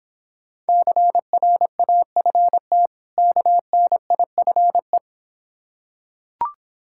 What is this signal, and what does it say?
Morse code, heard as a keyed tone, about 26 wpm, 690 Hz. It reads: CRAFT KNIFE